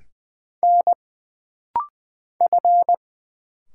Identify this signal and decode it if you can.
Morse code, heard as a keyed tone, about 20 wpm, 700 Hz. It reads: N F
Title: Single Letters - Rapid-Fire 20wpm